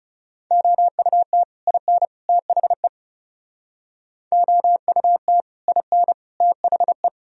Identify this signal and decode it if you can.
Morse code, heard as a keyed tone, about 39 wpm, 700 Hz. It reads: OUT IN THE OUT IN THE